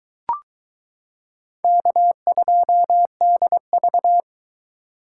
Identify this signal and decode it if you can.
Morse code, heard as a keyed tone, about 22 wpm, 700 Hz. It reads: K2DV